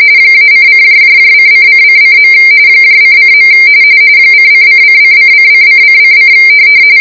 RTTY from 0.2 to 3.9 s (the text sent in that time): -:::::::. ::.